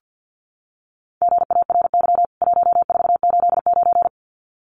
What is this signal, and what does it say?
Morse code, heard as a keyed tone, about 50 wpm, 700 Hz. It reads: GARY 1489
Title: CWT Exchange 50wpm